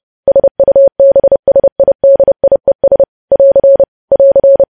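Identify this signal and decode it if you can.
Morse code, heard as a keyed tone, about 30 wpm, 550 Hz. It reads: SUBSIDIES <AR> <AR>
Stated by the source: Morse code practice of news headlines at 30.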